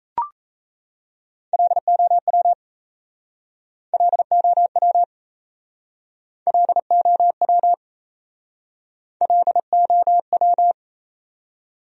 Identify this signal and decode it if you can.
Morse code, heard as a keyed tone, about 37 wpm, 690 Hz. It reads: LOW LOW LOW LOW